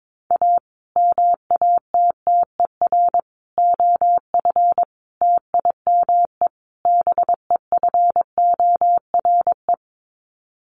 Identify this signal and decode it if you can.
Morse code, heard as a keyed tone, about 22 wpm, 700 Hz. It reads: A MATTER OF TIME BEFORE
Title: Sets of 5 Words Encore 22wpm